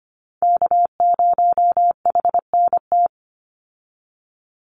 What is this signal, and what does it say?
Morse code, heard as a keyed tone, about 25 wpm, 700 Hz. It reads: K0HNT
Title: USA Call Signs 25wpm